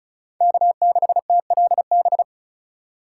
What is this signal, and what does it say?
Morse code, heard as a keyed tone, about 34 wpm, 700 Hz. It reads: K6TLB
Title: USA Call Signs - 5 Characters - 2x Character Spacing - 35wpm